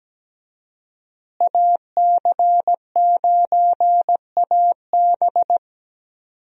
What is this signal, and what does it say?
Morse code, heard as a keyed tone, about 17 wpm, 700 Hz. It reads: AC9AB
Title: USA Call Signs - 5 Characters - Repeated 2x - 17wpm